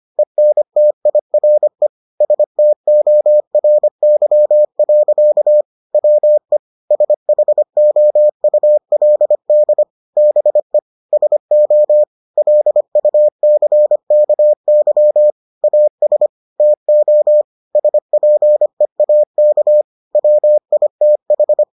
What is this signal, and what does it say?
Morse code, heard as a keyed tone, about 25 wpm, 600 Hz. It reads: ENTIRE STORY. WE SHOULD BE SO LUCKY AS TO SPEAK WITH